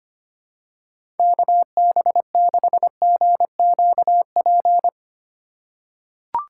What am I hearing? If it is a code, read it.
Morse code, heard as a keyed tone, about 25 wpm, 700 Hz. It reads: KB6GQP
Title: USA Call Signs - 6 Characters - 6x Character Spacing - 25wpm